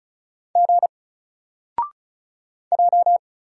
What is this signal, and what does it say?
Morse code, heard as a keyed tone, about 36 wpm, 700 Hz. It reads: G J
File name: Single-Letter-Number-35wpm